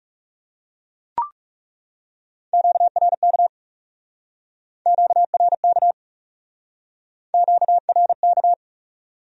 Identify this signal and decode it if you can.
Morse code, heard as a keyed tone, about 39 wpm, 700 Hz. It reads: QRK QRK QRK